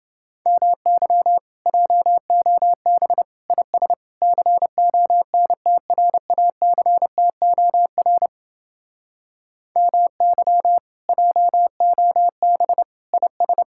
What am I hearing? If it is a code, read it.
Morse code, heard as a keyed tone, about 28 wpm, 700 Hz. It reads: MY JOB IS CONTRACTOR MY JOB IS